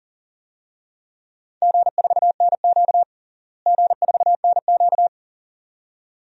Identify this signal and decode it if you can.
Morse code, heard as a keyed tone, about 40 wpm, 700 Hz. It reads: G4NQ G4NQ
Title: DXCC-100 Call Signs - 4 Characters - Repeated 2x - 40wpm